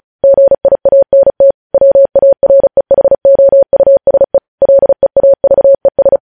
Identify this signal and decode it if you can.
Morse code, heard as a keyed tone, about 36 wpm, 550 Hz. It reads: GIANT WAREHOUSE LEAVES
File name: podcast / News headlines-1x-35x35